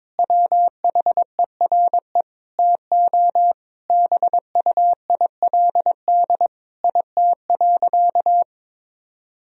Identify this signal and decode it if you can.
Morse code, heard as a keyed tone, about 22 wpm, 700 Hz. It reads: WHERE TO BUILD IT.
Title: Sentences from Top 1000 Words - Review Part 1 - 22wpm